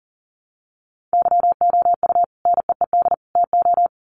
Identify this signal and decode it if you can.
Morse code, heard as a keyed tone, about 40 wpm, 700 Hz. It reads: YOU NEED TO
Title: Sets of 3 Words 40wpm